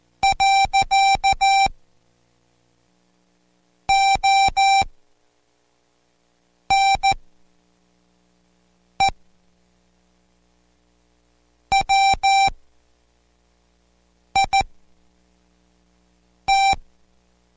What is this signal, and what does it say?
Morse code, heard as a keyed tone, about 14 wpm, spaced as for 4 wpm, 780 Hz. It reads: . ONE WIT